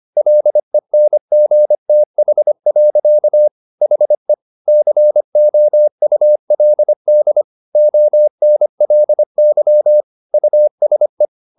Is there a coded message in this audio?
Yes: Morse code from 0.2 to 11.2 s, heard as a keyed tone, about 25 wpm, 600 Hz. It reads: LENGTH. HE COULD ONLY USE